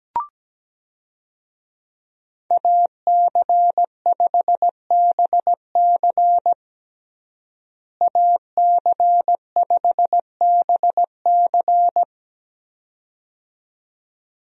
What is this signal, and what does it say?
Morse code, heard as a keyed tone, about 17 wpm, 700 Hz. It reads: AC5BC AC5BC
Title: USA Call Signs - 5 Characters - Repeated 2x - 17wpm